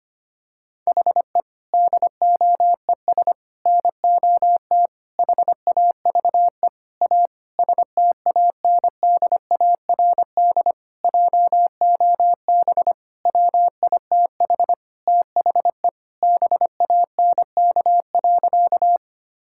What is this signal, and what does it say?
Morse code, heard as a keyed tone, about 25 wpm, 700 Hz. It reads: HE DOES NOT HAVE A STANDARD JOB WITH THE BANK.
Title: Sentences from Top 1000 Words - Review Part 6 - 25wpm